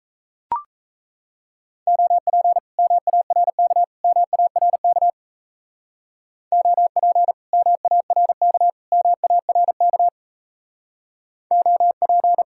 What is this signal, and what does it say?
Morse code, heard as a keyed tone, about 33 wpm, 700 Hz. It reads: OP MARK MARK OP MARK MARK OP
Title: Speed-Racing Common QSO Phrases Encore - 28wpm